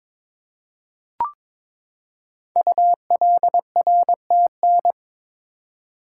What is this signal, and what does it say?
Morse code, heard as a keyed tone, about 22 wpm, 700 Hz. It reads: ULRTN
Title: Code Groups - 5 Letters - 22wpm